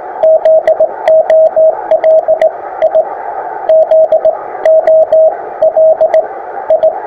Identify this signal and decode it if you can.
Morse code, heard as a keyed tone, about 18 wpm, spaced as for 15 wpm, 630 Hz, with radio noise. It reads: ZOLI ZOLI
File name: attachments / ha2rq 20m